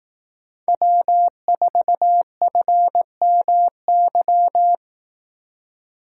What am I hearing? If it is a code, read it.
Morse code, heard as a keyed tone, about 18 wpm, 700 Hz. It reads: W4FMY
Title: Speed Racing - USA 1x3 Call Signs 15wpm